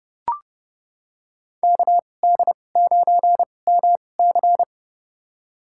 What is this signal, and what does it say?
Morse code, heard as a keyed tone, about 32 wpm, spaced as for 22 wpm, 700 Hz. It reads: KD9MC
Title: USA Call Signs - 5 Characters - 2x Character Spacing - 30wpm